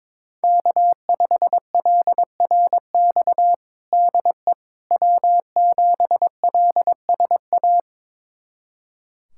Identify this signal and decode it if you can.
Morse code, heard as a keyed tone, about 22 wpm, 700 Hz. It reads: K5LRX DE W7LSA
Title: USA Callsign DE Callsign 22wpm